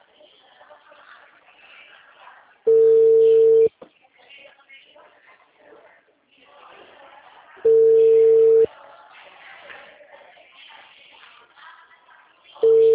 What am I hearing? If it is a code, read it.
Ringback tone (China, 450 Hz), with speech.